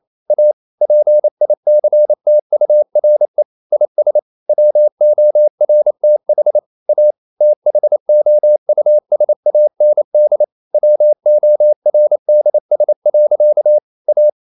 Morse code, heard as a keyed tone, about 28 wpm, 600 Hz. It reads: A PICTURE IS WORTH A THOUSAND WORDS. A
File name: CW / Practice / EnglishSayings28